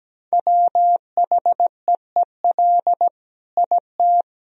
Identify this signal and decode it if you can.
Morse code, heard as a keyed tone, about 17 wpm, 700 Hz. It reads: WHEEL IT
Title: Sentences from Top 1000 Words - Review Part 3 - 17wpm